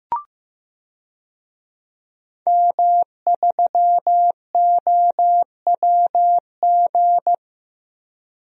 Morse code, heard as a keyed tone, about 15 wpm, 700 Hz. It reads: M3OWG
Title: DXCC-100 Call Signs - 5 Characters - 15wpm